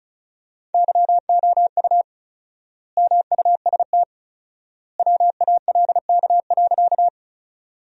Morse code, heard as a keyed tone, about 34 wpm, 700 Hz. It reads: YOU MUST WALK.
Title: 4x Word-Spacing - Sentences from Top 300 Words - 35wpm